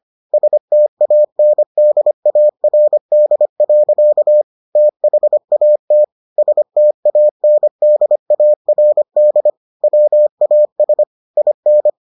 Morse code, heard as a keyed tone, about 25 wpm, 600 Hz. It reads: STANDARD. THAT STANDARD WAS IN